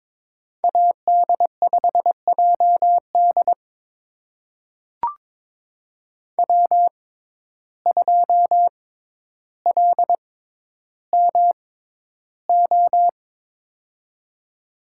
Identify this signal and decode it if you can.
Morse code, heard as a keyed tone, about 22 wpm, spaced as for 11 wpm, 700 Hz. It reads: AD5JD W2LMO
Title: USA Call Signs - 5 Characters - 6x Character Spacing - 22wpm